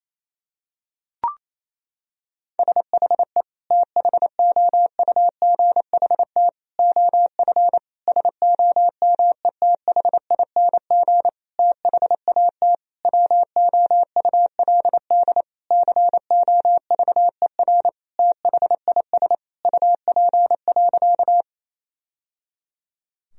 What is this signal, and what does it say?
Morse code, heard as a keyed tone, about 28 wpm, 700 Hz. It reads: SHE THOUGHT OF SOMETHING THAT WOULD COVER THIS UP.
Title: Sentences from Top 1000 Words - Review Part 1 - 28wpm